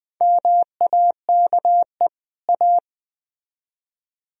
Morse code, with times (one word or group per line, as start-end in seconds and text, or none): 0.2-2.1 MAKE
2.5-2.8 A